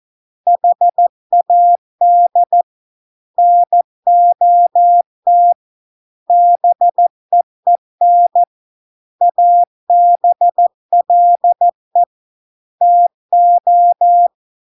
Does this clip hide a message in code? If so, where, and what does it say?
Yes: Morse code from 0.5 to 14.3 s, heard as a keyed tone, about 14 wpm, 700 Hz. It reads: HAD NOT BEEN ABLE TO